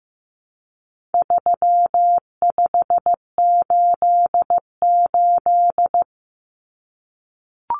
Morse code, heard as a keyed tone, about 15 wpm, 700 Hz. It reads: 3588
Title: Common Cut Numbers - 4-Digit 15wpm